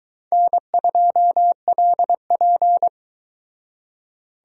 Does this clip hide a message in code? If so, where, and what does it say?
Yes: Morse code from 0.3 to 2.9 s, heard as a keyed tone, about 21 wpm, 700 Hz. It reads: N2LP